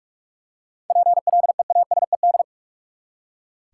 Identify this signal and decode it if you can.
Morse code, heard as a keyed tone, about 43 wpm, 700 Hz. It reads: PLEASED
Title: Seven Letter Words - 45wpm